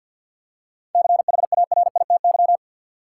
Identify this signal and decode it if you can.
Morse code, heard as a keyed tone, about 50 wpm, 700 Hz. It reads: CHARITY
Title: Seven Letter Words - 50wpm